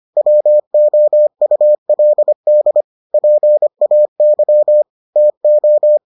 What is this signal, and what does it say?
Morse code, heard as a keyed tone, about 25 wpm, 600 Hz. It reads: WOULD PAY TO